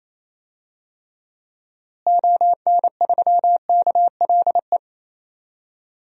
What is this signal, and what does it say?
Morse code, heard as a keyed tone, about 27 wpm, 700 Hz. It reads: ON3KLE
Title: DXCC-100 Call Signs - 6 Characters - Repeated 2x - 28wpm